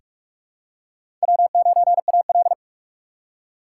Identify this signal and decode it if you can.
Morse code, heard as a keyed tone, about 46 wpm, 700 Hz. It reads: W9AL
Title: USA Call Signs - 4 Characters - 2x Character Spacing - 45wpm